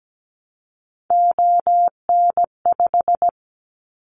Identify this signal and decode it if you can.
Morse code, heard as a keyed tone, about 17 wpm, 700 Hz. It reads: ON5
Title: Callsign Prefixes for Top 50 DXCC - 17wpm